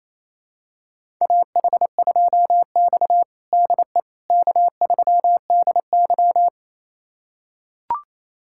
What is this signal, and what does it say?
Morse code, heard as a keyed tone, about 27 wpm, 700 Hz. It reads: AH2X DE K3DY
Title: USA Callsign DE Callsign - 4 Characters - 28wpm